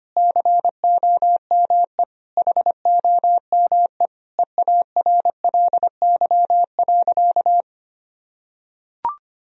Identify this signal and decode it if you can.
Morse code, heard as a keyed tone, about 25 wpm, 700 Hz. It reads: COME HOME EARLY.
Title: Sentences from Top 1000 Words - Review Part 4 - 25wpm